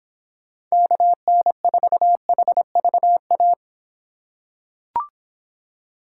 Morse code, heard as a keyed tone, about 25 wpm, 700 Hz. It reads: KN4HVA